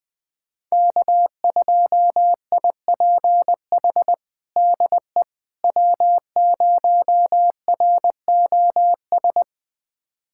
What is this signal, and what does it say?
Morse code, heard as a keyed tone, about 20 wpm, 700 Hz. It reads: K2IPH DE W0ROS
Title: USA Callsign DE Callsign - 5 Characters - 20wpm